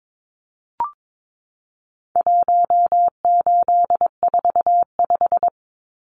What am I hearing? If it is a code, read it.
Morse code, heard as a keyed tone, about 22 wpm, 700 Hz. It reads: 1845
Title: Numbers - 4-Digit 22wpm